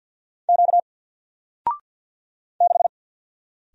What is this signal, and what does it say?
Morse code, heard as a keyed tone, about 50 wpm, 700 Hz. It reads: Q 6
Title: Single Character (Letter:Number:Punctuation:Prosign) 50wpm